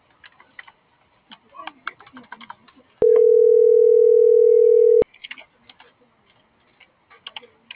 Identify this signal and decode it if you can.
Ringback tone (North America), 440 + 480 Hz.